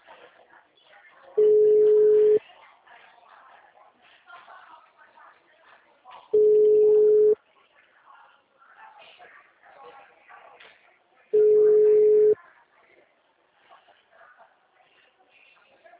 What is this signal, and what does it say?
Ringback tone (Europe, 425 Hz), with speech.